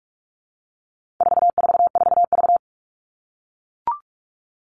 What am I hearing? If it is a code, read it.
Morse code, heard as a keyed tone, about 44 wpm, 700 Hz. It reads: 444V
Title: All Cut Numbers - 4-Digit 45wpm